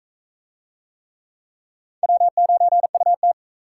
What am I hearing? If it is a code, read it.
Morse code, heard as a keyed tone, about 40 wpm, 700 Hz. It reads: W9UT